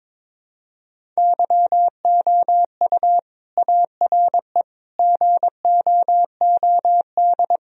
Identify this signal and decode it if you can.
Morse code, heard as a keyed tone, about 22 wpm, 700 Hz. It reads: YOU ARE GOOD
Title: Sentences from Top 1000 Words - Review Part 1 - 22wpm